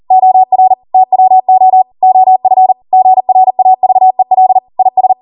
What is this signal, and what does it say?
Morse code, heard as a keyed tone, about 40 wpm, 750 Hz. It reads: OR TWO OF GRAVEL IS
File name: Morse / 40WPM